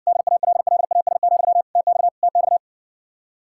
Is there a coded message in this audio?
Yes: Morse code from 0.1 to 2.6 s, heard as a keyed tone, about 60 wpm, 700 Hz. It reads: DALLAS, TX TX